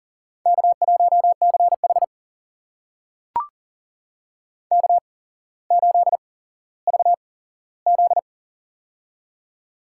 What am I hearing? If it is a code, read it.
Morse code, heard as a keyed tone, about 39 wpm, spaced as for 17 wpm, 700 Hz. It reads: K1CH K8VZ
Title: USA Call Signs - 4 Characters - 8x Character Spacing - 40wpm